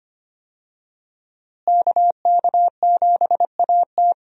Morse code, heard as a keyed tone, about 25 wpm, 700 Hz. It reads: KK7AT
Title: USA Call Signs - 5 Characters - Repeated 3x - 25wpm